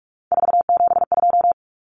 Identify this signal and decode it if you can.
Morse code, heard as a keyed tone, about 44 wpm, 700 Hz. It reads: 472
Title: Numbers - 3-Digit 45wpm